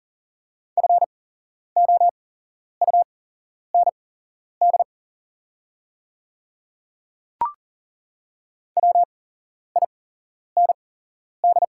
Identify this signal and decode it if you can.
Morse code, heard as a keyed tone, about 40 wpm, spaced as for 11 wpm, 690 Hz. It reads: FOUND WIND
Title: 8x Character Spacing - Top 500 Words - 40wpm